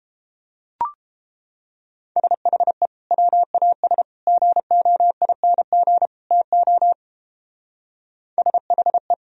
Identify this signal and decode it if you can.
Morse code, heard as a keyed tone, about 31 wpm, 700 Hz. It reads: SHE WAS GOING TO SHE